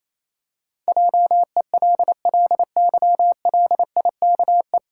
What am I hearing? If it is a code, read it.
Morse code, heard as a keyed tone, about 28 wpm, 700 Hz. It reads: JELLYLIKE
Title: Pangram Challenge v2 - 28wpm